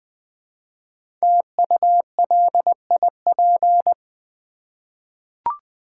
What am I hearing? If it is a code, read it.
Morse code, heard as a keyed tone, about 20 wpm, 700 Hz. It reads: TULIP
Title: Easy Words with Hints - 20wpm